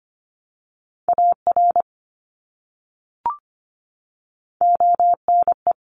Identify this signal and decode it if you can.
Morse code, heard as a keyed tone, about 25 wpm, 700 Hz. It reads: AR ONE
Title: ARRL Sections 25wpm